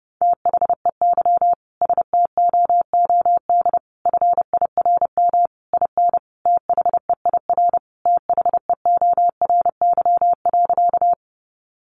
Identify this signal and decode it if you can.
Morse code, heard as a keyed tone, about 30 wpm, 700 Hz. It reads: THEY STOOD FIRM IN THEIR THEORY.
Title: Sentences from Top 1400 Words 30wpm